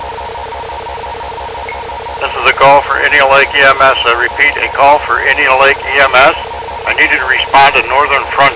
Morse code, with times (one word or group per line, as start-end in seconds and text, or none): none